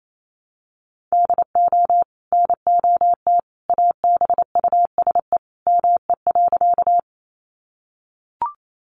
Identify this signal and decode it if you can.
Morse code, heard as a keyed tone, about 28 wpm, 700 Hz. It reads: DO NOT ABUSE ME.
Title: Sentences from Top 1300 Words 28wpm